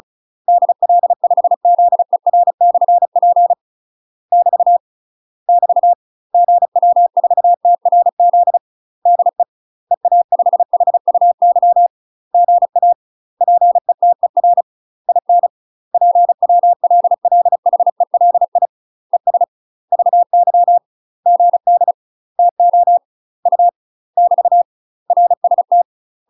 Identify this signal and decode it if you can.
Morse code, heard as a keyed tone, about 34 wpm, 700 Hz. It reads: DL5ZER/P <BT> <BT> GW4TRZ DE EA5HUY GA PETER IN PWLLHELI ES VY GD TO U <BT> RST
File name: UK MASTER QSOs TRAINING / 35WPM / UK MASTER QSOs 0040